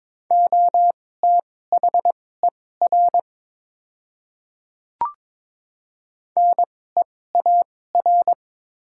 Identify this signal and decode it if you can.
Morse code, heard as a keyed tone, about 22 wpm, spaced as for 16 wpm, 700 Hz. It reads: OTHER NEAR
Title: 2x Character Spacing - Top 300 Words - 22wpm